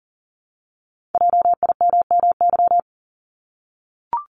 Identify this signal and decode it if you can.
Morse code, heard as a keyed tone, about 40 wpm, 700 Hz. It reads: JIMMY
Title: Multi-letter Phonemes Repeated 3x 40wpm